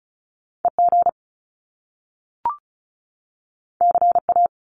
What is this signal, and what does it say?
Morse code, heard as a keyed tone, about 36 wpm, 700 Hz. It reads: EG CA
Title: Top Two Letter Combinations in Words 35wpm